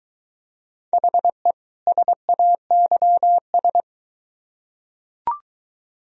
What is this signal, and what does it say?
Morse code, heard as a keyed tone, about 22 wpm, 700 Hz. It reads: HE SAYS